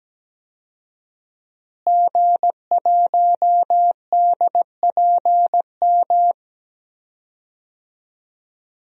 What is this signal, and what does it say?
Morse code, heard as a keyed tone, about 17 wpm, 700 Hz. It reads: G1DPM
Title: DXCC-100 Call Signs - 5 Characters - Repeated 3x - 17wpm